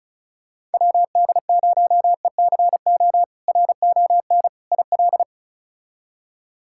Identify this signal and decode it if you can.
Morse code, heard as a keyed tone, about 36 wpm, 700 Hz. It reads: WD0ECO RON IL
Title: K1USN SST QSO Practice - 35wpm